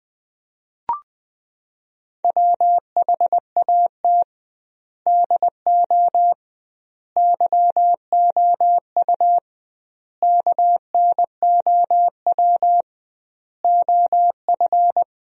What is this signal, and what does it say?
Morse code, heard as a keyed tone, about 20 wpm, 700 Hz. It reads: WHAT DO YOU KNOW OF
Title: 2x Word-Spacing - Sentences from Top 100 Words - 20wpm